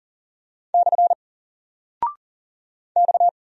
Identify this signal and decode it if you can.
Morse code, heard as a keyed tone, about 40 wpm, 700 Hz. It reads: / X
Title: Single Character (Letter:Number:Punctuation:Prosign) 40wpm